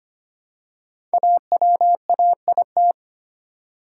Morse code, heard as a keyed tone, about 25 wpm, 700 Hz. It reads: AWAIT
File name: 5 Letter Words - First 3 Letters Given - 25wpm